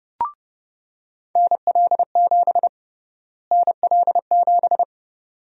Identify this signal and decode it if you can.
Morse code, heard as a keyed tone, about 29 wpm, 700 Hz. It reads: NL7 NL7
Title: USA Call Sign Prefixes - Repeated 2x - 30wpm